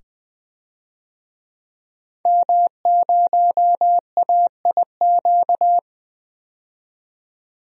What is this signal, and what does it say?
Morse code, heard as a keyed tone, about 20 wpm, 700 Hz. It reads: M0AIQ
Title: DXCC-100 Call Signs - 5 Characters - 20wpm